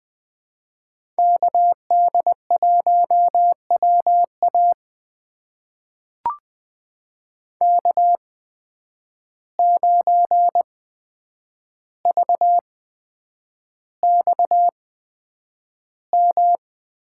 Morse code, heard as a keyed tone, about 20 wpm, spaced as for 9 wpm, 700 Hz. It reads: KD1WA K9VXM
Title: USA Call Signs - 5 Characters - 8x Character Spacing - 20wpm